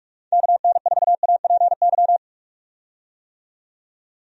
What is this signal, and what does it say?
Morse code, heard as a keyed tone, about 46 wpm, 700 Hz. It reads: KN4APY
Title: USA Call Signs - 6 Characters - 45wpm